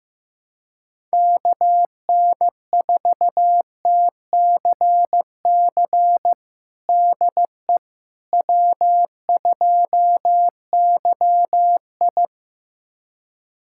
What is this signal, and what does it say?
Morse code, heard as a keyed tone, about 15 wpm, 700 Hz. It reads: KN4TCC DE W2YI
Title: USA Callsign DE Callsign 15wpm